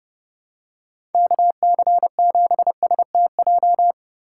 Morse code, heard as a keyed tone, about 30 wpm, 700 Hz. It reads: KC7STJ
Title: USA Call Signs - 4x Character Spacing - 30wpm